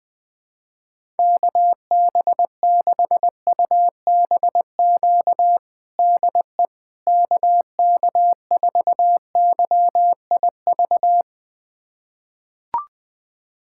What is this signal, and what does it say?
Morse code, heard as a keyed tone, about 21 wpm, 700 Hz. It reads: KB6UBQ DE KK4YIV
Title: USA Callsign DE Callsign - 6 Characters - 20wpm